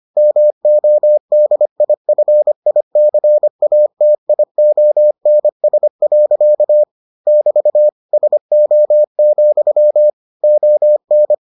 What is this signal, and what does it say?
Morse code, heard as a keyed tone, about 25 wpm, 600 Hz. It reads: MODIFICATIONS. <BT> SO, ON